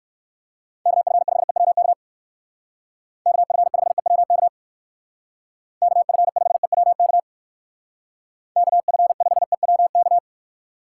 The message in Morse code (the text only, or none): KF5EWK KF5EWK KF5EWK KF5EWK